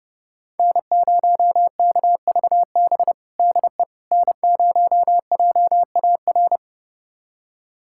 Morse code, heard as a keyed tone, about 30 wpm, 700 Hz. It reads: N0KVB DE N0JAR
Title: USA Callsign DE Callsign - 5 Characters - 30wpm